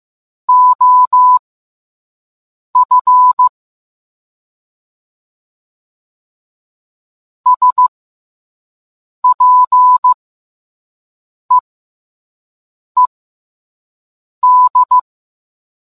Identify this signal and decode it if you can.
Morse code, heard as a keyed tone, about 15 wpm, spaced as for 5 wpm, 1000 Hz. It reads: OF SPEED